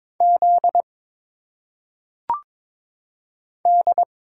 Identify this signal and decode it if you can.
Morse code, heard as a keyed tone, about 22 wpm, 700 Hz. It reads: Z D
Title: Single Letters - Phonetic Pronunciation 22wpm